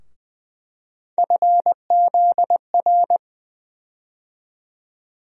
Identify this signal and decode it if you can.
Morse code, heard as a keyed tone, about 20 wpm, 700 Hz. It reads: FZR
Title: USA Call Sign Postfixes - 3 Letters 20wpm